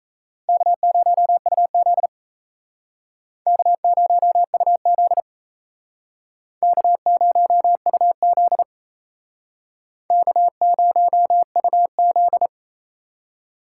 Morse code, heard as a keyed tone, about 36 wpm, 700 Hz. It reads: K0UZ K0UZ K0UZ K0UZ